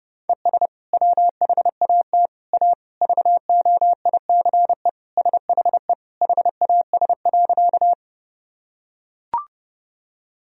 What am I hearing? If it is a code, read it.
Morse code, heard as a keyed tone, about 30 wpm, 700 Hz. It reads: ES WHAT A VOICE SHE HAS.
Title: Sentences from Top 500 Words and Abbreviations 30wpm